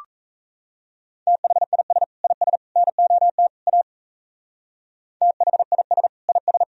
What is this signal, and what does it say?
Morse code, heard as a keyed tone, about 34 wpm, 690 Hz. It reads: THIS IS NOT A THIS IS
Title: Speed-Racing - Sets of 4 Words 28wpm